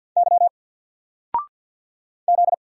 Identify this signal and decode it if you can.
Morse code, heard as a keyed tone, about 50 wpm, 700 Hz. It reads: Y Z
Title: Single Letter-Number 50wpm